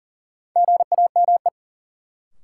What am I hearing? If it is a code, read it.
Morse code, heard as a keyed tone, about 40 wpm, 700 Hz. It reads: GAME
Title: Sentences from Top 500 Words - New Word Drill - 40wpm